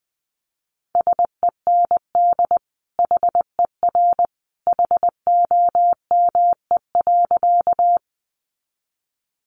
Morse code, heard as a keyed tone, about 20 wpm, 700 Hz. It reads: SEND HER HOME.
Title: Sentences from Top 1500 Words 20wpm